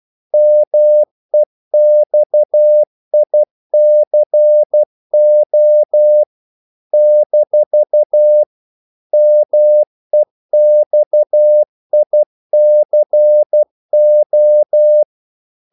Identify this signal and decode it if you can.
Morse code, heard as a keyed tone, about 12 wpm, 600 Hz. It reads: MEXICO - MEXICO
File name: CW / Practice / NationalCapitals12